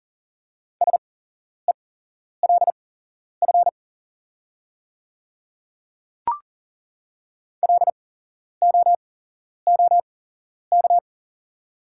Morse code, heard as a keyed tone, about 39 wpm, spaced as for 11 wpm, 700 Hz. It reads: SELF LOOK
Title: 8x Character Spacing - Top 200 Words - 40wpm